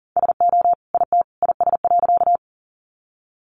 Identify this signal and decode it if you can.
Morse code, heard as a keyed tone, about 40 wpm, 700 Hz. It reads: SO IT IS.
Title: Sentences from Top 100 Words 40wpm